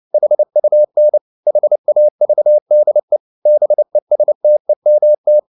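Morse code, heard as a keyed tone, about 29 wpm, 600 Hz. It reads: HUN HAVDE BESTEMT
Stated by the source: Fast (QRQ) Morse at 29 wpm, from CW med Gnister.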